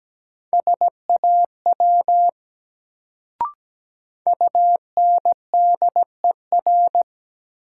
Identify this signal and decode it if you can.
Morse code, heard as a keyed tone, about 17 wpm, 700 Hz. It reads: SAW UNDER
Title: Sentences from Top 300 Words - New Word Drill - 17wpm